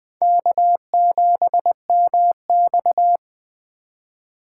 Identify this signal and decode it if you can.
Morse code, heard as a keyed tone, about 21 wpm, 700 Hz. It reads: K7MX